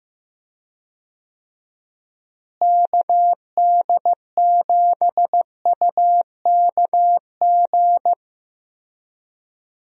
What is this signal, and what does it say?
Morse code, heard as a keyed tone, about 15 wpm, 700 Hz. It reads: KD7UKG
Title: USA Call Signs - 6 Characters - Repeated 2x - 15wpm